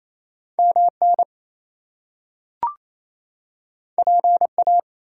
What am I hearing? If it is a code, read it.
Morse code, heard as a keyed tone, about 28 wpm, 700 Hz. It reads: MN PA
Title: US States and Canadian Provinces 28wpm